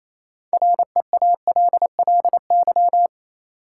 Morse code, heard as a keyed tone, about 28 wpm, 700 Hz. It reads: REALLY
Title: Sentences from Top 600 Words - New Word Drill - 28wpm